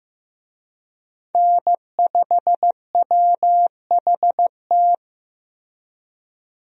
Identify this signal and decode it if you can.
Morse code, heard as a keyed tone, about 16 wpm, 700 Hz. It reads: N5WHT
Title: USA Call Signs - 5 Characters - Repeated 2x - 15wpm